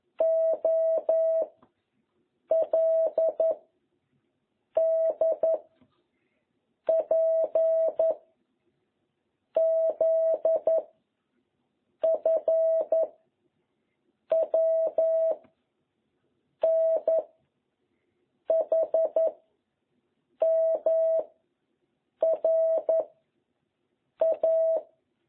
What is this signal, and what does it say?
Morse code, heard as a keyed tone, about 11 wpm, 660 Hz. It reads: O L D P Z F W N H M R A